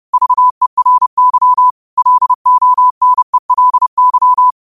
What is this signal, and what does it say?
Morse code, heard as a keyed tone, about 30 wpm, 1000 Hz. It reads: VERY LONELY